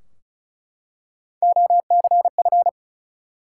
Morse code, heard as a keyed tone, about 36 wpm, 700 Hz. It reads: OCF
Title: Top 180 QSO Elements 35wpm